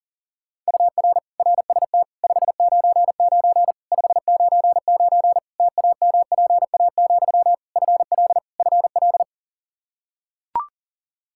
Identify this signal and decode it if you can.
Morse code, heard as a keyed tone, about 40 wpm, 700 Hz. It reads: UR RST 599 599 TAMPA, FL FL
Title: K1USN SST QSO Practice 40wpm